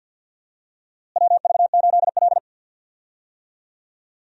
Morse code, heard as a keyed tone, about 47 wpm, 700 Hz. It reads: WV8L
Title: USA Call Signs - 4 Characters - Repeated 2x - 50wpm